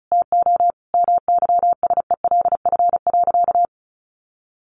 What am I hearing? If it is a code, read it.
Morse code, heard as a keyed tone, about 34 wpm, 700 Hz. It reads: TO MYSELF.
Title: Sentences from Top 1300 Words 35wpm